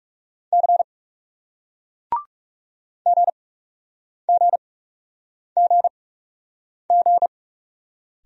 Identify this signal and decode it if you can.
Morse code, heard as a keyed tone, about 40 wpm, spaced as for 8 wpm, 700 Hz. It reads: C GGGG